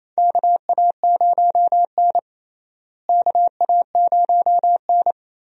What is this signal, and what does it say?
Morse code, heard as a keyed tone, about 28 wpm, 700 Hz. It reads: KA0N KA0N
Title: USA Call Signs - 4 Characters - Repeated 3x - 28wpm